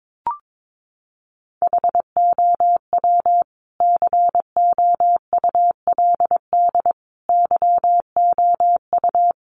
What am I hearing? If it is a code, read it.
Morse code, heard as a keyed tone, about 22 wpm, 700 Hz. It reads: HOW COULD YOU
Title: Sentences from Top 1300 Words 22wpm